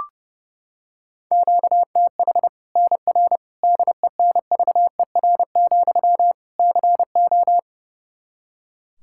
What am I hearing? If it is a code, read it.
Morse code, heard as a keyed tone, about 32 wpm, 700 Hz. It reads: QTH NR DENVER, CO
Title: K1USN SST QSO Practice 30wpm